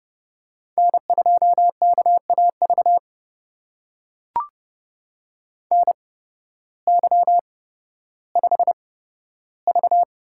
Morse code, heard as a keyed tone, about 30 wpm, spaced as for 14 wpm, 700 Hz. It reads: N2KAV NY5V
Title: USA Call Signs - 8x Character Spacing - 30wpm